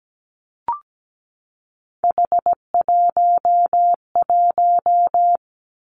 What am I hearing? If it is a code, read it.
Morse code, heard as a keyed tone, about 17 wpm, 700 Hz. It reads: H11
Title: Three Challenging Letter-Number Combinations 17wpm